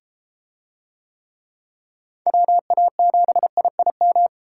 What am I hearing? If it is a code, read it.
Morse code, heard as a keyed tone, about 31 wpm, 700 Hz. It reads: WA7IIM